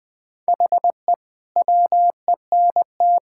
Morse code, heard as a keyed tone, about 21 wpm, 700 Hz. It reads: HE WENT
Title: Sentences from Top 1000 Words - Review Part 5 - 20wpm